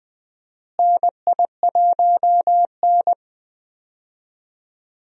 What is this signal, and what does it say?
Morse code, heard as a keyed tone, about 20 wpm, 700 Hz. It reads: NI1N
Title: USA Call Signs - 4 Characters - Repeated 3x - 20wpm